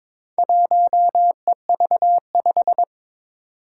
Morse code, heard as a keyed tone, about 22 wpm, 700 Hz. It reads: 1EV5
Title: Four Challenging Letter-Number Combinations 22wpm